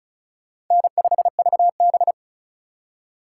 Morse code, heard as a keyed tone, about 36 wpm, 700 Hz. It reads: N5VB
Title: USA Call Signs - 4 Characters - 2x Character Spacing - 35wpm